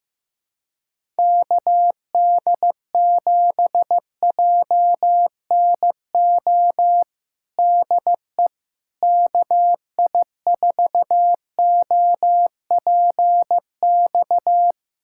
Morse code, heard as a keyed tone, about 15 wpm, 700 Hz. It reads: KD7JNO DE KI4OPX
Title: USA Callsign DE Callsign 15wpm